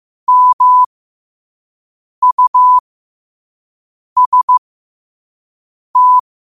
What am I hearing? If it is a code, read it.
Morse code, heard as a keyed tone, about 15 wpm, spaced as for 5 wpm, 1000 Hz. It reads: MUST
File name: qotdfiles / 5wpm